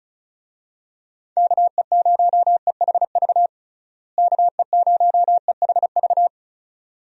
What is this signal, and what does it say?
Morse code, heard as a keyed tone, about 36 wpm, 700 Hz. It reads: KE0EHV KE0EHV
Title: USA Call Signs - 6 Characters - Repeated 2x - 35wpm